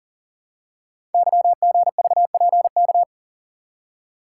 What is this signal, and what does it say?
Morse code, heard as a keyed tone, about 38 wpm, 690 Hz. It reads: YGVPK
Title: Code Groups - 5 Letters - 40wpm